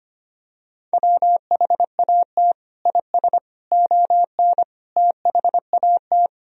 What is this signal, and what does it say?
Morse code, heard as a keyed tone, about 25 wpm, 700 Hz. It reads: WHAT IS ON THAT
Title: Sentences from Top 1000 Words - Review Part 4 - 25wpm